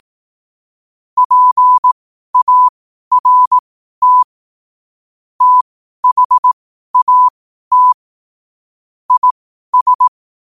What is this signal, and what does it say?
Morse code, heard as a keyed tone, about 18 wpm, spaced as for 12 wpm, 1000 Hz. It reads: PART THAT IS
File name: qotdfiles / 12wpm